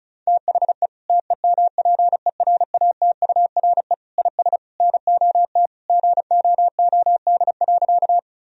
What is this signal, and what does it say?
Morse code, heard as a keyed tone, about 36 wpm, 700 Hz. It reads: THE TEMPERATURE IS NOT GOOD.
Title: Sentences from Top 1000 Words - Review Part 4 - 35wpm